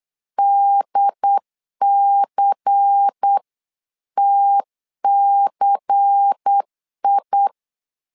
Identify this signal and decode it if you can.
Morse code, heard as a keyed tone, about 8 wpm, 800 Hz. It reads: DCTCI
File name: LEZIONI / 5 LBVC / 5 LBVC 8wpm 2935